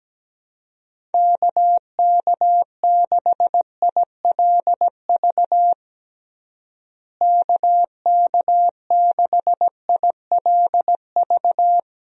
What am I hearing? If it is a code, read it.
Morse code, heard as a keyed tone, about 17 wpm, 700 Hz. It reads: KK6ILV KK6ILV
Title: USA Call Signs - 6 Characters - Repeated 2x - 17wpm